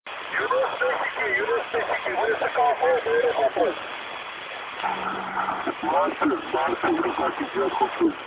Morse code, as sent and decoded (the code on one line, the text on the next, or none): none